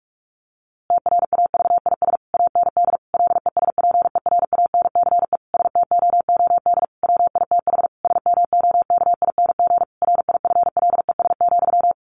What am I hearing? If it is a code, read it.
Morse code, heard as a keyed tone, about 44 wpm, 700 Hz. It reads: TRAVIS AND LESPERANCE STOOD WITH SMOKING RIFLES,
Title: A Sound Of Thunder 45wpm